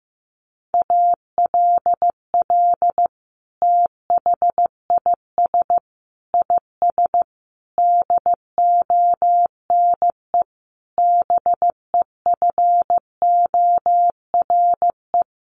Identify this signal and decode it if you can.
Morse code, heard as a keyed tone, about 14 wpm, 700 Hz. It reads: ALL THIS IS DONE BEFORE
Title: Sentences from Top 1500 Words 15wpm